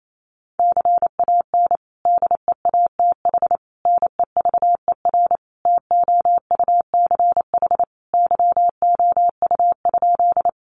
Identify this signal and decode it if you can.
Morse code, heard as a keyed tone, about 28 wpm, 700 Hz. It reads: CAN DEATH NEVER TOUCH YOU?
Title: Sentences from Top 1200 Words 28wpm